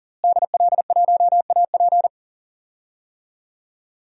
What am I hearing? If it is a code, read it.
Morse code, heard as a keyed tone, about 40 wpm, 700 Hz. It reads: DL1AP
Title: DXCC-100 Call Signs - 5 Characters - Repeated 2x - 40wpm